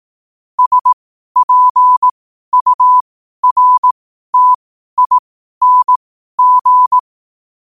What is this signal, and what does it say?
Morse code, heard as a keyed tone, about 18 wpm, spaced as for 12 wpm, 1000 Hz. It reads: SPURTING
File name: qotdfiles / 12wpm